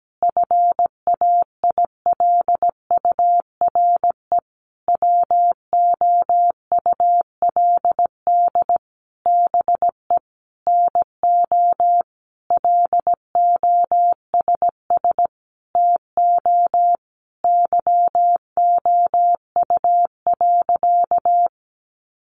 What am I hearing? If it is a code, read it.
Morse code, heard as a keyed tone, about 17 wpm, 700 Hz. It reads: FAILURE WOULD BE NO LOSS TO YOU.
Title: Sentences from Top 1400 Words 17wpm